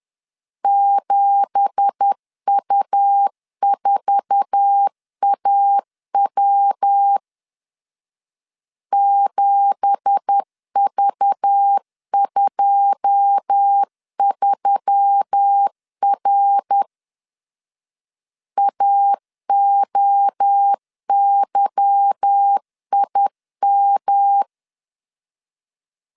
Morse code, heard as a keyed tone, about 10 wpm, 800 Hz. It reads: 7U4AW 7V23R AOYIM
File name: LEZIONI / 9 2867 10wpm 2733